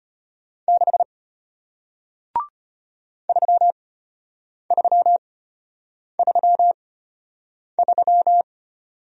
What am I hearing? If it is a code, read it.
Morse code, heard as a keyed tone, about 34 wpm, spaced as for 8 wpm, 700 Hz. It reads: 6 3333